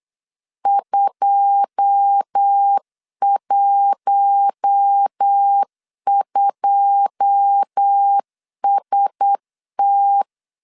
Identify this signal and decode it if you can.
Morse code, heard as a keyed tone, about 8 wpm, 800 Hz. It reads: 212ST
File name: LEZIONI / 9 2 8wpm 3437